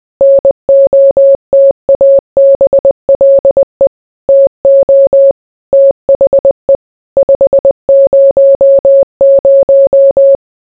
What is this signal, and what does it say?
Morse code, heard as a keyed tone, about 20 wpm, 550 Hz. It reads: NOTABLE TO THE 500